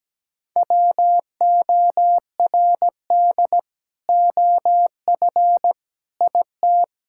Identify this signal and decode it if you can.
Morse code, heard as a keyed tone, about 17 wpm, 700 Hz. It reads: WORD OF IT
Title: Sentences from Top 1600 Words 17wpm